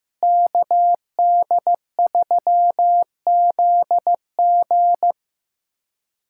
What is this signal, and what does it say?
Morse code, heard as a keyed tone, about 15 wpm, 700 Hz. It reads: KD3ZG
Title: USA Callsign DE Callsign - 5 Characters - 15wpm